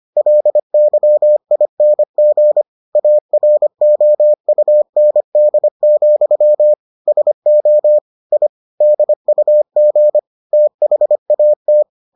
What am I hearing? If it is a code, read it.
Morse code, heard as a keyed tone, about 25 wpm, 600 Hz. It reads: LYING AROUND, SO I DUG THAT